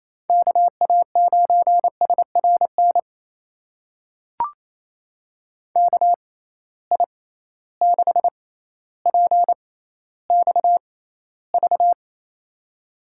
Morse code, heard as a keyed tone, about 28 wpm, spaced as for 14 wpm, 700 Hz. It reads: KA9SRN KI6PXV
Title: USA Call Signs - 6 Characters - 6x Character Spacing - 28wpm